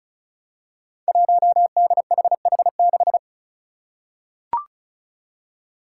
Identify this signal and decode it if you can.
Morse code, heard as a keyed tone, about 36 wpm, 700 Hz. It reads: 1DHH6
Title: Five Challenging Letter-Number Combinations 35wpm